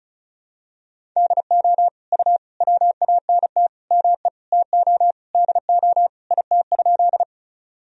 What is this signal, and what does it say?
Morse code, heard as a keyed tone, about 36 wpm, 700 Hz. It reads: DO U WANT ME TO DO IT?
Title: Sentences from Top 200 Words and Abbreviations 35wpm